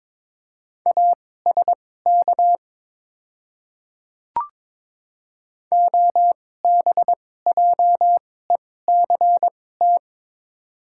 Morse code, heard as a keyed tone, about 22 wpm, spaced as for 16 wpm, 700 Hz. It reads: ASK OBJECT
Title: 2x Character Spacing - Top 500 Words - 22wpm